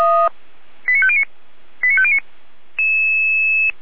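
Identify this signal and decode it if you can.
Five-tone selective call (ZVEI), heard as quick runs of five tones, tones of about 70 ms. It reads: 89509 89509
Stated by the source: ZVEI five-tone selective call received on SDR.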